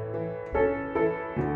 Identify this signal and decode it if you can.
There is no code.